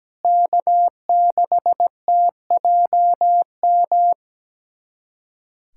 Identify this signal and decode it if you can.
Morse code, heard as a keyed tone, about 17 wpm, 700 Hz. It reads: K6TJM
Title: USA Callsign DE Callsign 17wpm